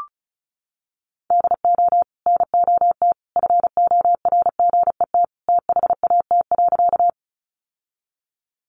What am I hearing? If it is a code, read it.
Morse code, heard as a keyed tone, about 34 wpm, 700 Hz. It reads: DO NOT FORGET THAT.
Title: Sentences from Top 1200 Words 35wpm